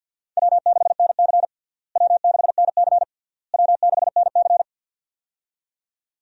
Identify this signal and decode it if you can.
Morse code, heard as a keyed tone, about 51 wpm, 700 Hz. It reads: W6NC W6NC W6NC
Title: USA Call Signs - 4 Characters - Repeated 3x - 50wpm